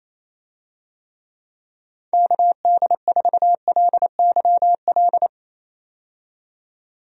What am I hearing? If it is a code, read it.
Morse code, heard as a keyed tone, about 27 wpm, 700 Hz. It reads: KD4LYL
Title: USA Call Signs - 6 Characters - 28wpm